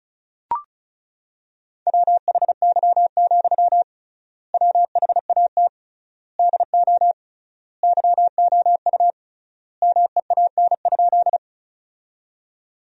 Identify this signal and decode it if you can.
Morse code, heard as a keyed tone, about 35 wpm, 700 Hz. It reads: WHY, WHAT DO YOU MEAN?
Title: 3x Word-Spacing - Sentences from Top 200 Words - 35wpm